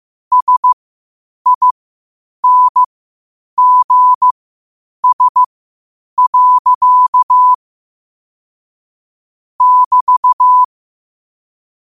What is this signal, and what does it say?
Morse code, heard as a keyed tone, about 15 wpm, spaced as for 8 wpm, 1000 Hz. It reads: SINGS. <BT>